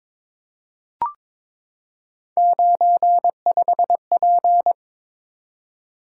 Morse code, heard as a keyed tone, about 22 wpm, 700 Hz. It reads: 95P
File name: practice / 22wpm / Three Letter-Number Combinations - 22wpm